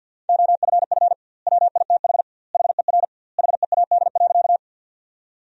Morse code, heard as a keyed tone, about 51 wpm, 700 Hz. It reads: OFF WITH HER HEAD.
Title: 2x Word-Spacing - Sentences from Top 300 Words - 50wpm